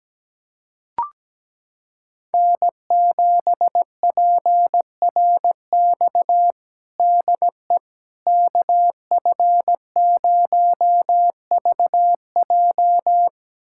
Morse code, heard as a keyed tone, about 17 wpm, 700 Hz. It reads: N7PRX DE KF0VJ
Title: USA Callsign DE Callsign - 5 Characters - 17wpm